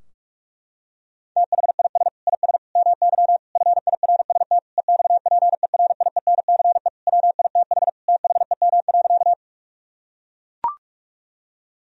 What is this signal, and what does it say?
Morse code, heard as a keyed tone, about 41 wpm, 700 Hz. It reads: THIS IS MY FIRST EXPERIENCE WITH THEM.
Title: Sentences from Top 1000 Words - Review Part 3 - 45wpm